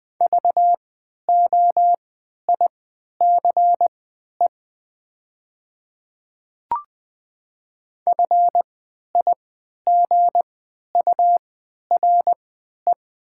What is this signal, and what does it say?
Morse code, heard as a keyed tone, about 20 wpm, spaced as for 11 wpm, 700 Hz. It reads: VOICE FIGURE
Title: 3x Character Spacing - Top 500 Words - 20wpm